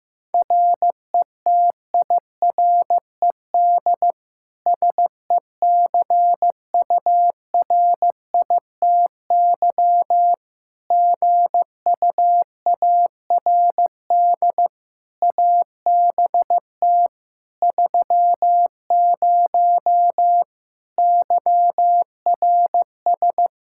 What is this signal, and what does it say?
Morse code, heard as a keyed tone, about 15 wpm, 700 Hz. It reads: RETIRED SECURITY GUARD ABT 30 YRS
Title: K1USN SST QSO Practice 15wpm